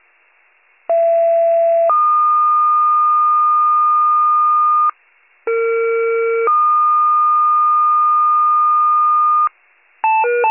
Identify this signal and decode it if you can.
Two-tone page, heard as a one-second tone, then a three-second tone, about 670 then 1220 Hz, about 460 then 1220 Hz. It is heard from 0.9 to 9.5 s.